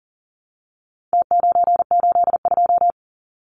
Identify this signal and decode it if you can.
Morse code, heard as a keyed tone, about 40 wpm, 700 Hz. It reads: T982
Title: Numbers - Contest Style - 4-Digit 40wpm